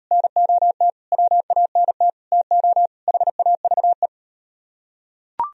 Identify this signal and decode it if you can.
Morse code, heard as a keyed tone, about 38 wpm, 700 Hz. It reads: NOT WANT TO HAVE